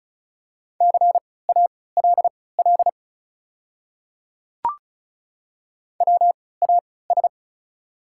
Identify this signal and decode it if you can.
Morse code, heard as a keyed tone, about 36 wpm, spaced as for 20 wpm, 700 Hz. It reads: CALL WAS
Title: 3x Character Spacing - Top 200 Words - 35wpm